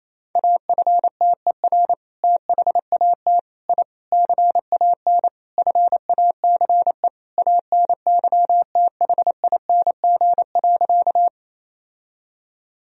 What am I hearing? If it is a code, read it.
Morse code, heard as a keyed tone, about 28 wpm, 700 Hz. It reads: AFTER THAT I CAN FACE ANYTHING.
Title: Sentences from Top 1000 Words - Review Part 2 - 28wpm